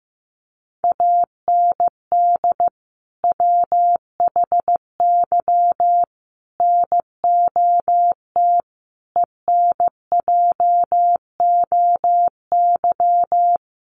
Morse code, heard as a keyed tone, about 14 wpm, 700 Hz. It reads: AND WHY NOT ENJOY
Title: Sentences from Top 1300 Words 15wpm